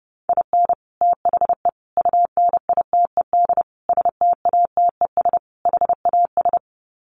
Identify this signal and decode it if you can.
Morse code, heard as a keyed tone, about 30 wpm, 700 Hz. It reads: IN THE UNITED STATES HAS
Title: Sets of 5 Words Encore2 30wpm